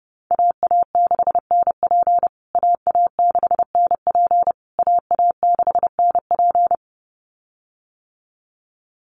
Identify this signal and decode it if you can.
Morse code, heard as a keyed tone, about 30 wpm, 700 Hz. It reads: AA6NP AA6NP AA6NP
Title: USA Call Signs Repeated 3x 30wpm